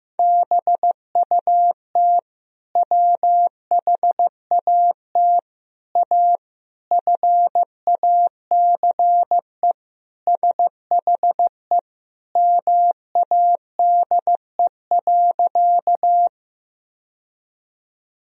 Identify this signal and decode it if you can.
Morse code, heard as a keyed tone, about 15 wpm, 700 Hz. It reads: BUT WHAT A FACE SHE MADE.
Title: Sentences from Top 1000 Words - Review Part 2 - 15wpm